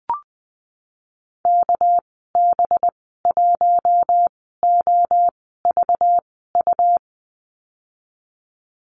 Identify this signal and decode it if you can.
Morse code, heard as a keyed tone, about 20 wpm, spaced as for 15 wpm, 700 Hz. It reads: KB1OVU
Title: 231 Lesson 37g - Letter x - Call Signs 20wpm